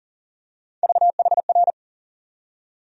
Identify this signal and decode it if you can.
Morse code, heard as a keyed tone, about 40 wpm, 700 Hz. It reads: VHR
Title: USA Call Sign Postfixes - 3 Letters 40wpm